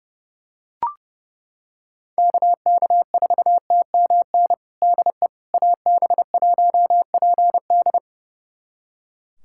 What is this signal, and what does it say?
Morse code, heard as a keyed tone, about 30 wpm, 700 Hz. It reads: KK4TMN DE AB1PD
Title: USA Callsign DE Callsign 30wpm